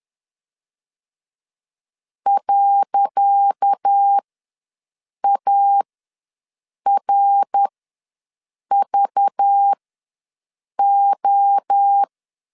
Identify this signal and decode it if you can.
Morse code, heard as a keyed tone, about 10 wpm, spaced as for 6 wpm, 800 Hz. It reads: .ARVO
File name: LEZIONI / 10 SIGNS / 10C DOT ws 10wpm 4213